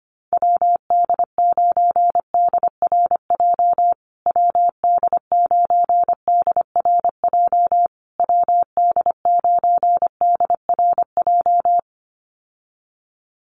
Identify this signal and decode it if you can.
Morse code, heard as a keyed tone, about 25 wpm, 700 Hz. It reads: WD9DRJ WD9DRJ WD9DRJ
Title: USA Call Signs Repeated 3x 25wpm